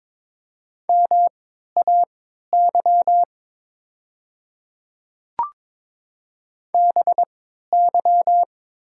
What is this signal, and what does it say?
Morse code, heard as a keyed tone, about 22 wpm, spaced as for 13 wpm, 700 Hz. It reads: MAY BY
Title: 3x Character Spacing - Top 100 Words - 22wpm